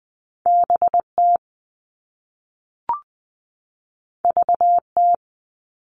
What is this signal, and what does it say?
Morse code, heard as a keyed tone, about 20 wpm, 700 Hz. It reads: BT VT
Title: All Cut Numbers - 2-Digit 20wpm